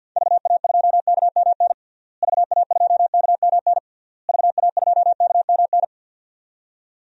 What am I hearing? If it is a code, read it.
Morse code, heard as a keyed tone, about 48 wpm, 700 Hz. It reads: VA2KMN VA2KMN VA2KMN
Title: DXCC-100 Call Signs - 6 Characters - Repeated 3x - 50wpm